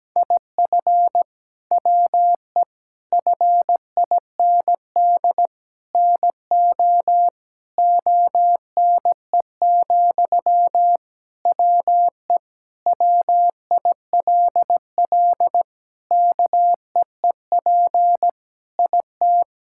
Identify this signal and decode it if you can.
Morse code, heard as a keyed tone, about 17 wpm, 700 Hz. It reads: IF WE FIND NO ONE, WE WILL KEEP IT
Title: Sentences from Top 1600 Words 17wpm